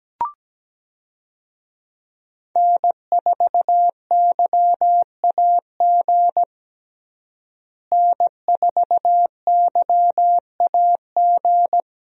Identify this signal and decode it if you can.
Morse code, heard as a keyed tone, about 17 wpm, 700 Hz. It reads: N4YAG N4YAG
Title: USA Call Signs - 5 Characters - Repeated 3x - 17wpm